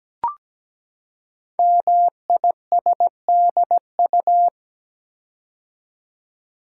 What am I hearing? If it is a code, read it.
Morse code, heard as a keyed tone, about 17 wpm, 700 Hz. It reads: MISDU
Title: Code Groups - 5 Letters - 17wpm